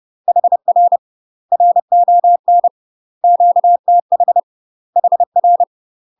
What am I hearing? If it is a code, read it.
Morse code, heard as a keyed tone, about 30 wpm, 700 Hz. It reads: HR RON QTH HR
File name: UK MASTER QSOs TRAINING / 30WPM / UK MASTER QSOs 0043